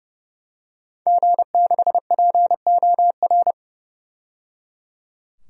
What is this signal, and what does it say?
Morse code, heard as a keyed tone, about 29 wpm, 700 Hz. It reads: G6POR
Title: DXCC-100 Call Signs - 5 Characters - 30wpm